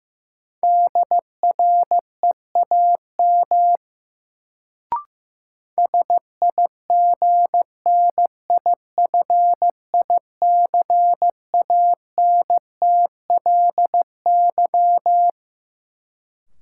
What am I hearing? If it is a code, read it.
Morse code, heard as a keyed tone, about 15 wpm, 700 Hz. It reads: DREAM SIGNIFICANTLY
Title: Sentences from Top 1500 Words - New Word Drill - 15wpm